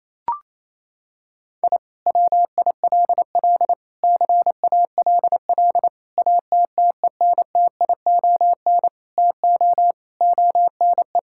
Morse code, heard as a keyed tone, about 28 wpm, 700 Hz. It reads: I WILL CALL ATTENTION TO ONE
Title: Sentences from Top 1000 Words - Review Part 6 - 28wpm